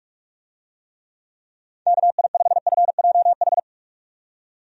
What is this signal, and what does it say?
Morse code, heard as a keyed tone, about 44 wpm, 700 Hz. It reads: KI5FJH
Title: USA Call Signs - 6 Characters - 45wpm